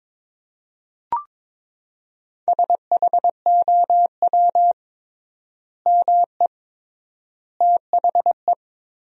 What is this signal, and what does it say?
Morse code, heard as a keyed tone, about 22 wpm, 700 Hz. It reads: SHOW ME THE
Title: 3x Word-Spacing - Sentences from Top 200 Words - 22wpm